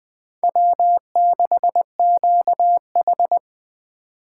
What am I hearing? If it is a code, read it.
Morse code, heard as a keyed tone, about 20 wpm, 700 Hz. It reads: W6QH
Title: Speed Racing - USA 1x2 Call Signs 17wpm